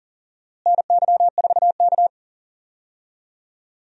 Morse code, heard as a keyed tone, about 38 wpm, 700 Hz. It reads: NY4K
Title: USA Call Signs - 4 Characters - Repeated 3x - 40wpm